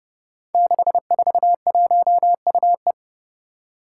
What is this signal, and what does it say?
Morse code, heard as a keyed tone, about 30 wpm, 700 Hz. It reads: 641UE
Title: Five Challenging Letter-Number Combinations 30wpm